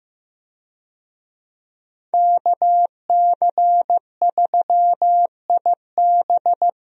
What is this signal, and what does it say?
Morse code, heard as a keyed tone, about 15 wpm, 700 Hz. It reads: KC3IB
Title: USA Call Signs - 5 Characters - Repeated 3x - 15wpm